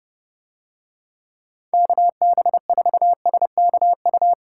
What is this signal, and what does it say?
Morse code, heard as a keyed tone, about 29 wpm, 700 Hz. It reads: KB4SKU